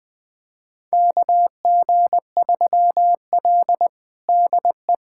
Morse code, heard as a keyed tone, about 20 wpm, 700 Hz. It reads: KG3L DE
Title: USA Callsign DE Callsign - 4 Characters - 20wpm